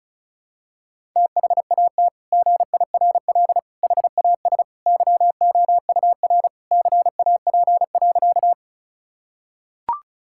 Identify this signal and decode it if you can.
Morse code, heard as a keyed tone, about 36 wpm, 700 Hz. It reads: THAT GIRL HAS YOUR CAP.
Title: Sentences from Top 1600 Words 35wpm